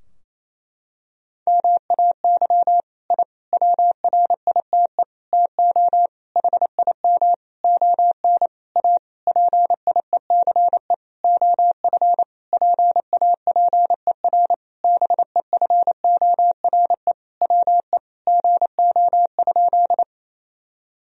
Morse code, heard as a keyed tone, about 28 wpm, 700 Hz. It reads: MAY I WRITE TO HIM ON A PIECE OF PAPER BEFORE WE GO?
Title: Sentences from Top 1000 Words - Review Part 2 - 28wpm